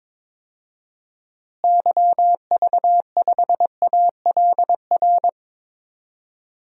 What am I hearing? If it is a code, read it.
Morse code, heard as a keyed tone, about 21 wpm, 700 Hz. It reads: YV5ALR
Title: DXCC-100 Call Signs - 6 Characters - Repeated 2x - 22wpm